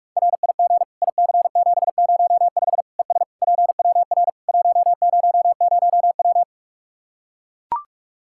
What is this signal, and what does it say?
Morse code, heard as a keyed tone, about 44 wpm, 700 Hz. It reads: RIG IC705 ES PWR 100W
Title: Practice QSO - Exchange 2 - Info - 45wpm